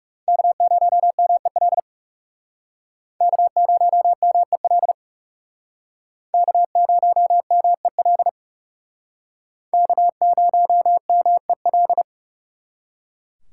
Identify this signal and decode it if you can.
Morse code, heard as a keyed tone, about 36 wpm, 700 Hz. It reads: K0MEL K0MEL K0MEL K0MEL
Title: Speed Racing - USA 1x3 Call Signs 30wpm